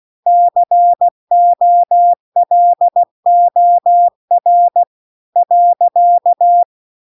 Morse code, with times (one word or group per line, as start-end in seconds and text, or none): 0.3-4.8 COLOR
5.4-6.6 .